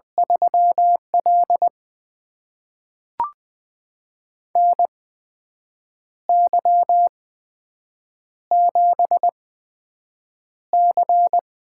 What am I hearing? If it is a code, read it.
Morse code, heard as a keyed tone, about 20 wpm, spaced as for 7 wpm, 700 Hz. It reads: 3L NY7C
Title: USA Call Signs - 4 Characters - 8x Character Spacing - 20wpm